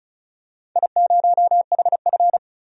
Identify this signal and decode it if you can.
Morse code, heard as a keyed tone, about 36 wpm, 700 Hz. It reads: I0HF
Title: DXCC-100 Call Signs - 4 Characters - 35wpm